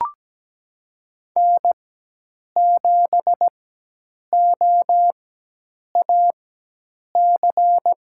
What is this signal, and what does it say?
Morse code, heard as a keyed tone, about 17 wpm, spaced as for 8 wpm, 700 Hz. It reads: N7OAC
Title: USA Call Signs - 5 Characters - 4x Character Spacing - 17wpm